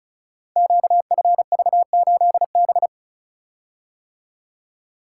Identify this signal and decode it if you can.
Morse code, heard as a keyed tone, about 36 wpm, 700 Hz. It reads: QFV8B
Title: Code Groups - 5 Letter:Numbers - 35wpm